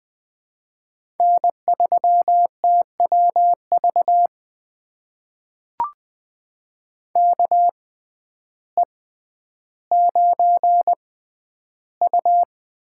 Morse code, heard as a keyed tone, about 20 wpm, spaced as for 11 wpm, 700 Hz. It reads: N3TWV KE9U
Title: USA Call Signs - 5 Characters - 6x Character Spacing - 20wpm